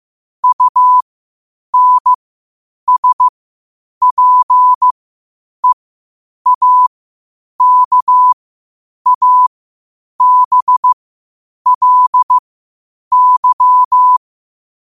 Morse code, heard as a keyed tone, about 15 wpm, spaced as for 8 wpm, 1000 Hz. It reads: UNSPEAKABLY
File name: qotdfiles / 8wpm